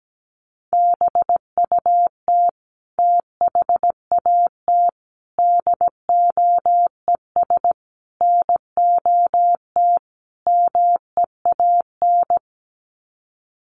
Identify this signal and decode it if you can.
Morse code, heard as a keyed tone, about 17 wpm, 700 Hz. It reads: BUT THAT DOES NOT MEAN
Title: Sets of 5 Words 17wpm